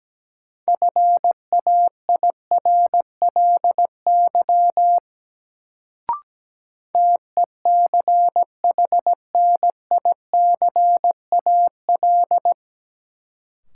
Morse code, heard as a keyed tone, about 17 wpm, 700 Hz. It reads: FAIRLY TECHNICAL
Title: Sentences from Top 1100 Words - New Word Drill - 17wpm